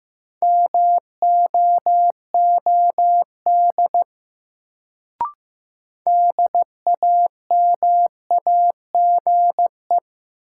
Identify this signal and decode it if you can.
Morse code, heard as a keyed tone, about 15 wpm, 700 Hz. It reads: MOOD DAMAGE
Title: Sentences from Top 1100 Words - New Word Drill Alt - 15wpm